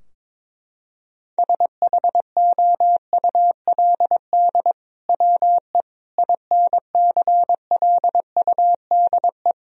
Morse code, heard as a keyed tone, about 23 wpm, 700 Hz. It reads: SHOULD WE INCLUDE
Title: Sentences from Top 1000 Words - Review Part 3 - 22wpm